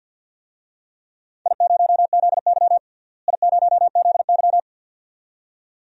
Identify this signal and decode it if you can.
Morse code, heard as a keyed tone, about 52 wpm, 700 Hz. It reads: I0ZY I0ZY
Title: DXCC-100 Call Signs - 4 Characters - Repeated 2x - 50wpm